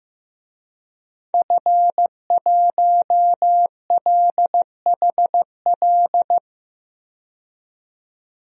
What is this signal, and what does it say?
Morse code, heard as a keyed tone, about 15 wpm, 700 Hz. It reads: F1LHL
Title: DXCC-100 Call Signs - 5 Characters - 15wpm